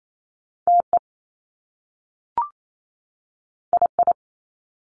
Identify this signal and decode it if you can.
Morse code, heard as a keyed tone, about 28 wpm, 700 Hz. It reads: TE II
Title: Top Two Letter Combinations in Words 28wpm